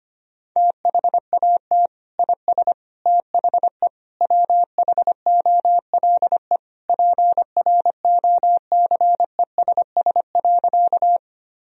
Morse code, heard as a keyed tone, about 25 wpm, 700 Hz. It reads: THAT IS THE WHOLE PROCESS.
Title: Sentences from Top 1000 Words - Review Part 3 - 25wpm